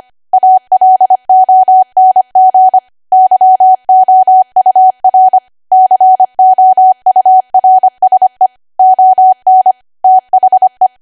Morse code, heard as a keyed tone, about 25 wpm, 750 Hz. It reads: ALONG YOUR COURSE ON THE
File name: Archive / 25 WPM / 25WPM